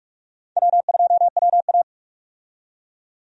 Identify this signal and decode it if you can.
Morse code, heard as a keyed tone, about 47 wpm, 700 Hz. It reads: W2WA
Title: USA Call Signs - 4 Characters - 45wpm